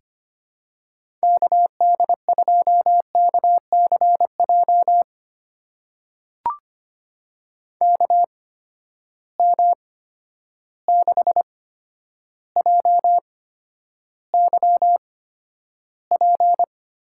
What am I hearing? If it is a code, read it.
Morse code, heard as a keyed tone, about 25 wpm, spaced as for 11 wpm, 700 Hz. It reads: KD2KCJ KM6JYP
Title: USA Call Signs - 6 Characters - 8x Character Spacing - 25wpm